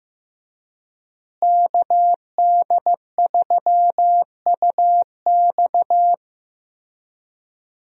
Morse code, heard as a keyed tone, about 15 wpm, 700 Hz. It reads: KD3UX
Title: USA Call Signs - 5 Characters - Repeated 3x - 15wpm